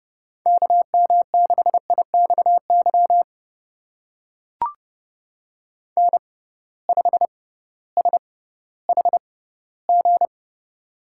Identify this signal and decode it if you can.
Morse code, heard as a keyed tone, about 30 wpm, spaced as for 16 wpm, 700 Hz. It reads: KM6IXY N5SHG
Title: USA Call Signs - 6x Character Spacing - 30wpm